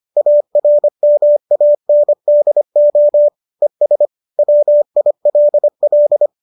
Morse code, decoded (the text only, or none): ARMANDO ES WILL